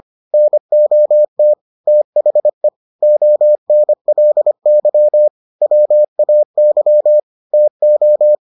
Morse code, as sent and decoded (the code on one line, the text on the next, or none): -. --- - / - .... / --- -. .-.. -.-- / .-- .- -.-- / - ---
NOT THE ONLY WAY TO